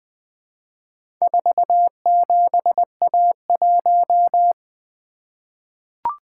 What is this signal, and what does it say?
Morse code, heard as a keyed tone, about 20 wpm, 700 Hz. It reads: 47A1
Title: Four Letter-Number Combinations - 20wpm